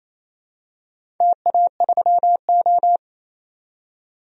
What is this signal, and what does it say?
Morse code, heard as a keyed tone, about 28 wpm, 700 Hz. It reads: TA3O
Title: DXCC-100 Call Signs - 4 Characters - 28wpm